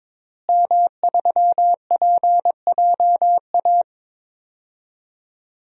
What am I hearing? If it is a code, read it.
Morse code, heard as a keyed tone, about 21 wpm, 700 Hz. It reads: M3PJA
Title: DXCC-100 Call Signs - 5 Characters - Repeated 2x - 22wpm